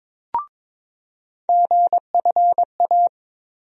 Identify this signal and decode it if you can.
Morse code, heard as a keyed tone, about 22 wpm, 700 Hz. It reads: GFA
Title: USA Call Sign Postfixes - 3 Letters 22wpm